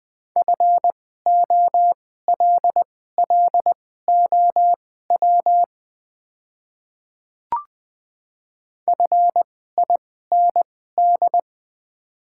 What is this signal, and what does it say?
Morse code, heard as a keyed tone, about 20 wpm, spaced as for 15 wpm, 690 Hz. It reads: FOLLOW FIND
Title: 2x Character Spacing - Top 200 Words - 20wpm